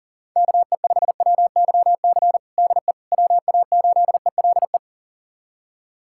Morse code, heard as a keyed tone, about 40 wpm, 700 Hz. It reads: KE5WYC DE WA8ELE
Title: USA Callsign DE Callsign - 6 Characters - 40wpm